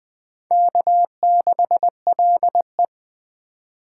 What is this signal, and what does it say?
Morse code, heard as a keyed tone, about 19 wpm, 700 Hz. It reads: K6LE